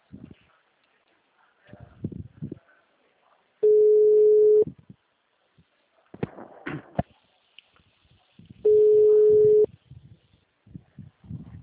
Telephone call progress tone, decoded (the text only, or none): ringback tone